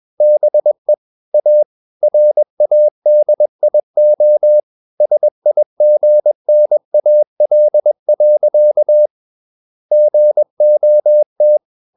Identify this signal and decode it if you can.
Morse code, heard as a keyed tone, about 21 wpm, 600 Hz. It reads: BE A RADIO SIGNAL. GOT
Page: CW med Gnister